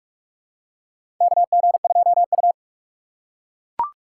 Morse code, heard as a keyed tone, about 42 wpm, 700 Hz. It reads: KG2U
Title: USA Call Signs - 4 Characters - 6x Character Spacing - 45wpm